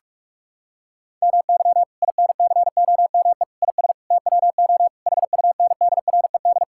Morse code, heard as a keyed tone, about 46 wpm, 700 Hz. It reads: MY INCOME IS TWO HUNDRED
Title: Sentences from Top 1000 Words - Review Part 5 - 45wpm